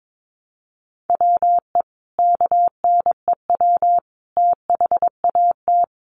Morse code, heard as a keyed tone, about 22 wpm, 700 Hz. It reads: WE KNEW THAT
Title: Sentences from Top 400 Words 22wpm